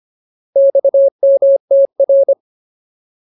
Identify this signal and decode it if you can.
Morse code, heard as a keyed tone, about 25 wpm, 550 Hz. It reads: XMTR
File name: Ultimate Word List / 25WPM Ultimate